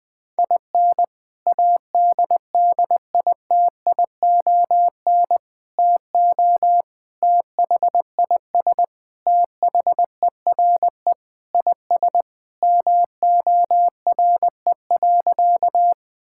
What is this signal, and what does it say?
Morse code, heard as a keyed tone, about 19 wpm, 700 Hz. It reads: IN ADDITION TO THIS THERE IS MORE.
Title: Sentences with 3-Word Phrases - 20wpm